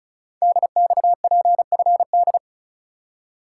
Morse code, heard as a keyed tone, about 36 wpm, 700 Hz. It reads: DXPFD
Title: Code Groups - 5 Letters - 35wpm